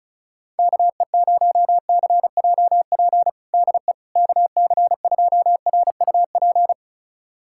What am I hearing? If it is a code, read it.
Morse code, heard as a keyed tone, about 36 wpm, 700 Hz. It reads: KE0CJP DE KC2RUP
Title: USA Callsign DE Callsign - 6 Characters - 35wpm